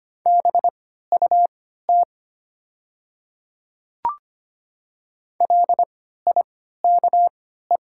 Morse code, heard as a keyed tone, about 25 wpm, spaced as for 14 wpm, 700 Hz. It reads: BUT LIKE
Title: 3x Character Spacing - Top 100 Words - 25wpm